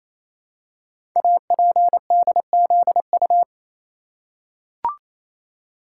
Morse code, heard as a keyed tone, about 28 wpm, 700 Hz. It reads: APDZU
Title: Code Groups - 5 Letters - 28wpm